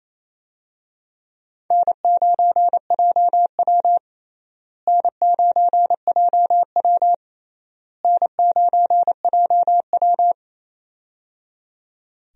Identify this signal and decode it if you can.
Morse code, heard as a keyed tone, about 28 wpm, 700 Hz. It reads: N9JW N9JW N9JW
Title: USA Call Signs - 4 Characters - Repeated 3x - 28wpm